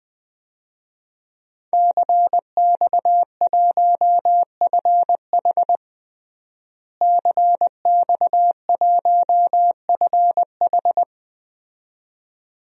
Morse code, heard as a keyed tone, about 21 wpm, 700 Hz. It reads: CX1FH CX1FH
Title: DXCC-100 Call Signs - 5 Characters - Repeated 2x - 20wpm